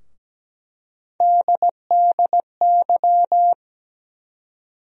Morse code, heard as a keyed tone, about 17 wpm, 700 Hz. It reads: DDY
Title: USA Call Sign Postfixes - 3 Letters 17wpm